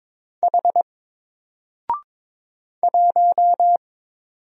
Morse code, heard as a keyed tone, about 22 wpm, 700 Hz. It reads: H 1
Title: Single Character (Letter:Number:Punctuation:Prosign) 22wpm